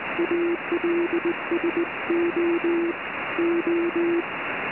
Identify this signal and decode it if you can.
Morse code, heard as a keyed tone, about 19 wpm, 380 Hz, with noise. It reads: ALSO O